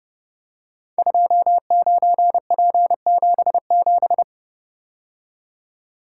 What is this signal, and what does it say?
Morse code, heard as a keyed tone, about 29 wpm, 700 Hz. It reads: 29P77
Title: Code Groups - 5 Letter:Numbers - 30wpm